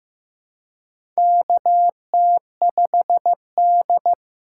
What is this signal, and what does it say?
Morse code, heard as a keyed tone, about 15 wpm, 700 Hz. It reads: KT5D
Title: USA Call Signs - 4 Characters - 3x Character Spacing - 15wpm